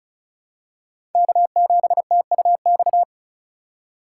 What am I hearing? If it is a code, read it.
Morse code, heard as a keyed tone, about 36 wpm, 700 Hz. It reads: K7TUX
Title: USA Call Signs - 5 Characters - 3x Character Spacing - 35wpm